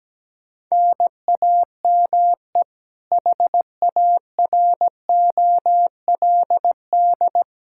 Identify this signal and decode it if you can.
Morse code, heard as a keyed tone, about 17 wpm, 700 Hz. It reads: NAME HAROLD
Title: Practice QSO - Exchange 1 - Beginning - 17wpm